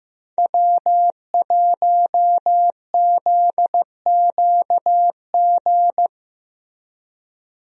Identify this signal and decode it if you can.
Morse code, heard as a keyed tone, about 16 wpm, 700 Hz. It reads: W1ZQG
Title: USA Call Signs - 5 Characters - Repeated 3x - 15wpm